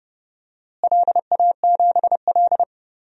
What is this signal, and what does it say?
Morse code, heard as a keyed tone, about 30 wpm, 700 Hz. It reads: LA7L
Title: DXCC-100 Call Signs - 4 Characters - Repeated 2x - 30wpm